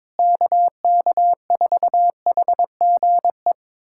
Morse code, heard as a keyed tone, about 21 wpm, 700 Hz. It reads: KK4HGE
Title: USA Callsign DE Callsign - 6 Characters - 22wpm